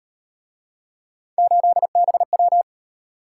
Morse code, heard as a keyed tone, about 35 wpm, 700 Hz. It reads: 8BW